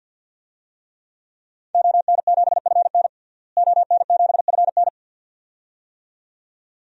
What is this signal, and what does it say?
Morse code, heard as a keyed tone, about 53 wpm, 700 Hz. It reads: ON7FN ON7FN
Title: DXCC-100 Call Signs - 5 Characters - Repeated 2x - 50wpm